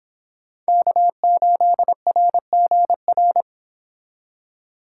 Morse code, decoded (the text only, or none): K8RGR